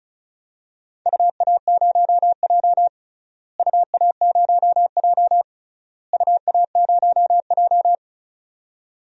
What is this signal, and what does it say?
Morse code, heard as a keyed tone, about 34 wpm, 700 Hz. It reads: UA0J UA0J UA0J
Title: DXCC-100 Call Signs - 4 Characters - Repeated 3x - 35wpm